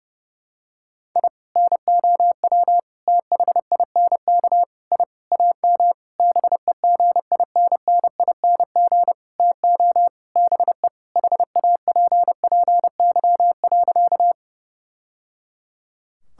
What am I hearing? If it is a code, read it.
Morse code, heard as a keyed tone, about 32 wpm, 700 Hz. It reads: I NOW THINK I AM BEGINNING TO BE HAPPY.
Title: Sentences from Top 1000 Words - Review Part 5 - 30wpm